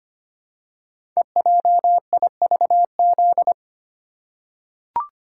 Morse code, heard as a keyed tone, about 25 wpm, 700 Hz. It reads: EJIVZ
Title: Code Groups - 5 Letters - 25wpm